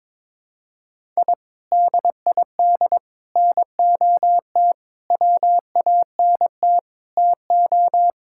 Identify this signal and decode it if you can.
Morse code, heard as a keyed tone, about 22 wpm, 700 Hz. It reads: I DID NOT WANT TO
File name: practice / 22wpm / Build A Phrase - 5-Word Phrases - 22wpm